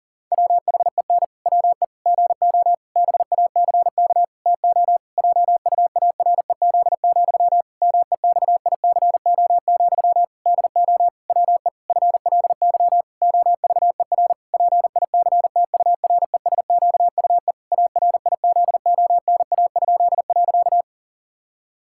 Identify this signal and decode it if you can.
Morse code, heard as a keyed tone, about 40 wpm, 700 Hz. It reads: WHEN WE GO BACK TO JUAREZ, MEXICO, DO WE FLY OVER PICTURESQUE ARIZONA?.
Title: Pangram Challenge v2 - 40wpm